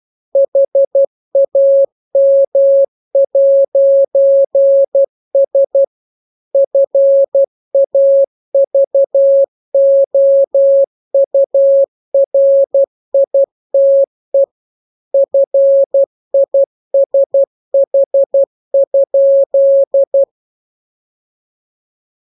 Morse code, heard as a keyed tone, about 12 wpm, 550 Hz. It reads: HAM'S FAVOURITE FISH?